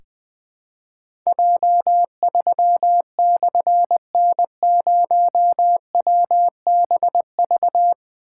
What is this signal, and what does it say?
Morse code, heard as a keyed tone, about 21 wpm, 700 Hz. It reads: J3/N0WBV
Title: Difficult Call Signs 20wpm